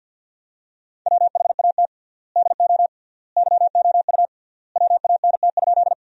Morse code, heard as a keyed tone, about 50 wpm, 700 Hz. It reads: WHAT DO YOU WANT?
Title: 3x Word-Spacing - Sentences from Top 200 Words - 50wpm